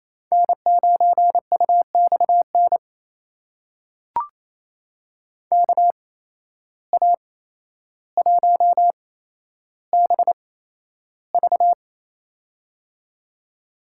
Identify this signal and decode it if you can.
Morse code, heard as a keyed tone, about 28 wpm, spaced as for 12 wpm, 700 Hz. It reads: N9UXN KA1BV
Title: USA Call Signs - 8x Character Spacing - 28wpm